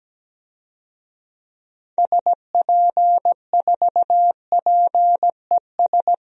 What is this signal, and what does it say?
Morse code, heard as a keyed tone, about 17 wpm, 700 Hz. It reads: SP4PES
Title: DXCC-100 Call Signs - 6 Characters - Repeated 2x - 17wpm